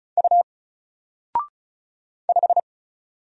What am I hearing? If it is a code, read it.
Morse code, heard as a keyed tone, about 36 wpm, 700 Hz. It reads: U 5
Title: Single Letter-Number Alt 35wpm